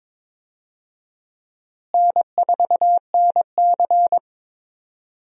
Morse code, heard as a keyed tone, about 22 wpm, 700 Hz. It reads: N4NC
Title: USA Call Signs - 4 Characters - Repeated 3x - 22wpm